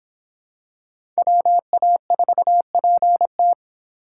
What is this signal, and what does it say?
Morse code, heard as a keyed tone, about 24 wpm, 700 Hz. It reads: WA4PT